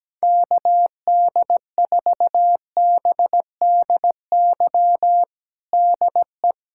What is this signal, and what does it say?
Morse code, heard as a keyed tone, about 17 wpm, 700 Hz. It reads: KD4BDY DE
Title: USA Callsign DE Callsign 17wpm